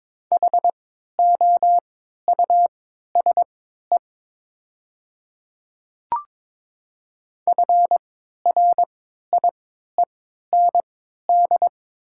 Morse code, heard as a keyed tone, about 22 wpm, spaced as for 13 wpm, 700 Hz. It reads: HOUSE FRIEND
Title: 3x Character Spacing - Top 300 Words - 22wpm